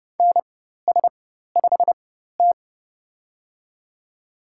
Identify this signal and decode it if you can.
Morse code, heard as a keyed tone, about 31 wpm, spaced as for 14 wpm, 700 Hz. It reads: NS5T
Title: USA Call Signs - 4 Characters - 4x Character Spacing - 30wpm